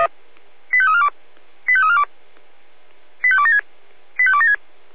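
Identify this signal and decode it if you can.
Five-tone selective call (ZVEI), heard as quick runs of five tones, tones of about 70 ms. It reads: 85342 85342 85276 85276
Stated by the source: ZVEI five-tone selective call received on SDR.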